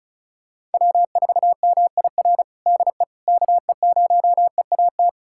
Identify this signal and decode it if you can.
Morse code, heard as a keyed tone, about 36 wpm, 700 Hz. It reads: W4MIR DE KE0EAT
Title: USA Callsign DE Callsign 35wpm